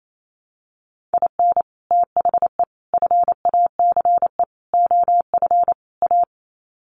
Morse code, heard as a keyed tone, about 28 wpm, 700 Hz. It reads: IN THE FACE OF A
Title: Sets of 5 Words Encore 28wpm